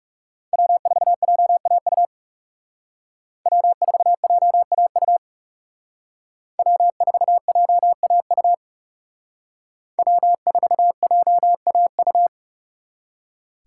Morse code, heard as a keyed tone, about 38 wpm, 700 Hz. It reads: W4JAU W4JAU W4JAU W4JAU